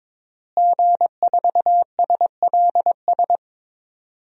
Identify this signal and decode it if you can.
Morse code, heard as a keyed tone, about 22 wpm, 700 Hz. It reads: G4SLS
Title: DXCC-100 Call Signs - 5 Characters - Repeated 3x - 22wpm